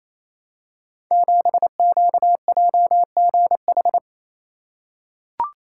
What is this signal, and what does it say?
Morse code, heard as a keyed tone, about 29 wpm, 700 Hz. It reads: 7QJGH
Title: Code Groups - 5 Letter:Numbers - 28wpm